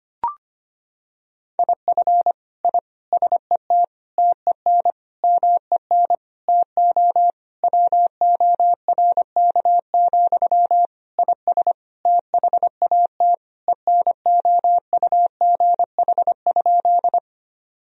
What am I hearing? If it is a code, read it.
Morse code, heard as a keyed tone, about 25 wpm, 700 Hz. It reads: IF I SET TEN MEN TO WORK, IS THAT ENOUGH?
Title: Sentences from Top 1000 Words - Review Part 2 - 25wpm